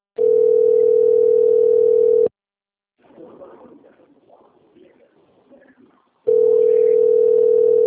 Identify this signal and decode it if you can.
Ringback tone (North America), 440 + 480 Hz.